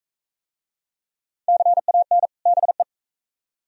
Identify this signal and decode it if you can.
Morse code, heard as a keyed tone, about 41 wpm, 700 Hz. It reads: CAN BE